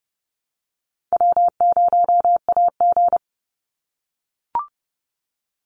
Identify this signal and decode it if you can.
Morse code, heard as a keyed tone, about 30 wpm, 700 Hz. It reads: W0AG
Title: Call Signs - 30wpm